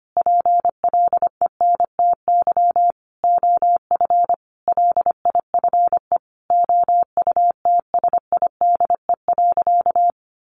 Morse code, heard as a keyed tone, about 25 wpm, 700 Hz. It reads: PLENTY OF LIFE OUTSIDE.
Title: Sentences from Top 1300 Words 25wpm